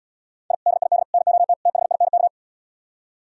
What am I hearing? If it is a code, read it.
Morse code, heard as a keyed tone, about 77 wpm, 700 Hz. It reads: I DID NOT THINK